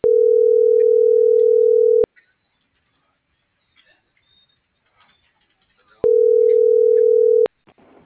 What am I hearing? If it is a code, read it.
Ringback tone (North America), 440 + 480 Hz.